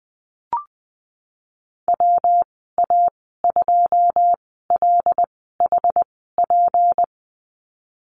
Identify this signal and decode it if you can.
Morse code, heard as a keyed tone, about 20 wpm, spaced as for 15 wpm, 700 Hz. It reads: WA2LHP
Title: 167 Lesson 27g - Number 7 - Call Signs 20wpm